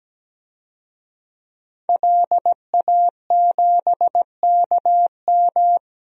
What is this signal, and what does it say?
Morse code, heard as a keyed tone, about 17 wpm, 700 Hz. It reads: LA7KM
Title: DXCC-100 Call Signs - 5 Characters - Repeated 2x - 17wpm